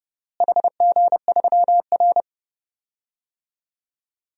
Morse code, heard as a keyed tone, about 30 wpm, 700 Hz. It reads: HG3R
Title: DXCC-100 Call Signs - 4 Characters - Repeated 2x - 30wpm